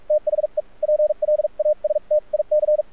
Morse code, heard as a keyed tone, about 46 wpm, spaced as for 38 wpm, 600 Hz. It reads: THE PLASTIC
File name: archived / CRSnet